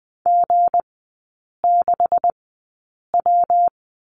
Morse code, heard as a keyed tone, about 20 wpm, spaced as for 8 wpm, 700 Hz. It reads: G6W
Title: 212 Lesson 34e - Number 8 - Three Characters 20wpm